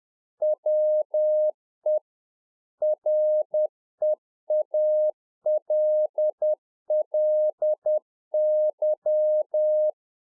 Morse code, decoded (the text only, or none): WE REALLY